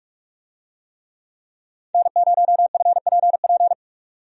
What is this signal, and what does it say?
Morse code, heard as a keyed tone, about 46 wpm, 700 Hz. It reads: N0FPP
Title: USA Call Signs - 5 Characters - 45wpm